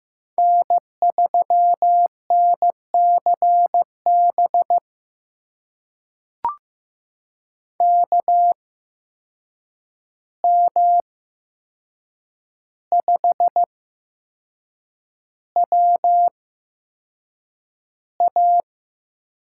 Morse code, heard as a keyed tone, about 15 wpm, spaced as for 6 wpm, 700 Hz. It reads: N3NCB KM5WA
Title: USA Call Signs - 5 Characters - 8x Character Spacing - 15wpm